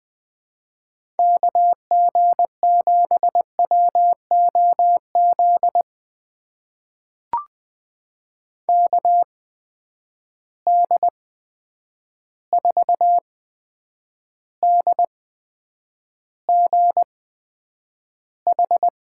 Morse code, heard as a keyed tone, about 20 wpm, spaced as for 9 wpm, 700 Hz. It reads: KG7WOZ KD4DGH
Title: USA Call Signs - 6 Characters - 8x Character Spacing - 20wpm